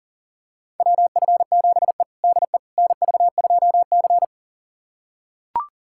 Morse code, heard as a keyed tone, about 40 wpm, 700 Hz. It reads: WF7E DE NV2C
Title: USA Callsign DE Callsign - 4 Characters - 40wpm